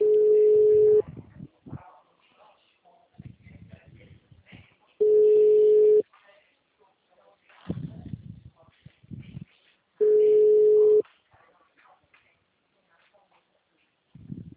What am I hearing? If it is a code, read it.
Ringback tone (Europe, 425 Hz), with speech.